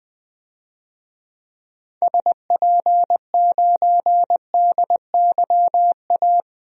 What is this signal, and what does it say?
Morse code, heard as a keyed tone, about 20 wpm, 700 Hz. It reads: SP9DYA
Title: DXCC-100 Call Signs - 6 Characters - 20wpm